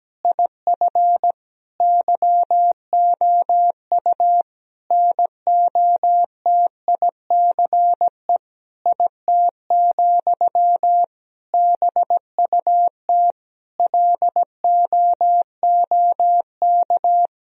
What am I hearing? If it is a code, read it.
Morse code, heard as a keyed tone, about 17 wpm, 700 Hz. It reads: IF YOU NOTICE IT, BUT LOOK
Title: Sentences from Top 1000 Words - Review Part 2 - 17wpm